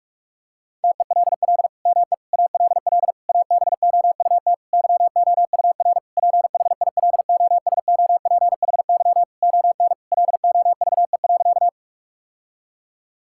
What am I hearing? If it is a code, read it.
Morse code, heard as a keyed tone, about 41 wpm, 700 Hz. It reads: TELL ME ALL ABOUT YOUR PHILOSOPHY ON LOVE.
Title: Sentences from Top 1000 Words - Review Part 6 - 45wpm